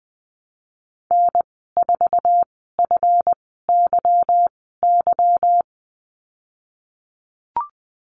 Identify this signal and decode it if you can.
Morse code, heard as a keyed tone, about 20 wpm, spaced as for 15 wpm, 700 Hz. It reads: N4FYY
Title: 193 Lesson 31g - Letter v - Call Signs 20wpm